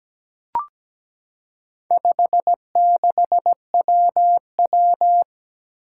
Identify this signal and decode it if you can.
Morse code, heard as a keyed tone, about 17 wpm, 700 Hz. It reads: 56WW
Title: Four Challenging Letter-Number Combinations 17wpm